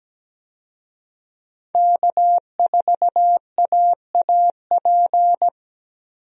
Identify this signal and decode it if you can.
Morse code, heard as a keyed tone, about 17 wpm, 700 Hz. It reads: K4AAP
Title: USA Call Signs - 5 Characters - 17wpm